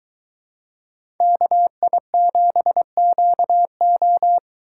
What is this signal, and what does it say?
Morse code, heard as a keyed tone, about 22 wpm, 700 Hz. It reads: KI7QO